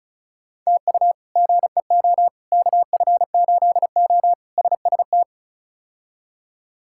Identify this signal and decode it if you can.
Morse code, heard as a keyed tone, about 36 wpm, 700 Hz. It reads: TU GEO KF8O SST
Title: K1USN SST QSO Practice - 35wpm